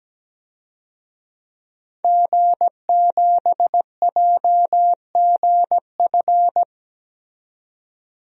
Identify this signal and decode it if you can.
Morse code, heard as a keyed tone, about 17 wpm, 700 Hz. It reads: G7JGF
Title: DXCC-100 Call Signs - 5 Characters - 17wpm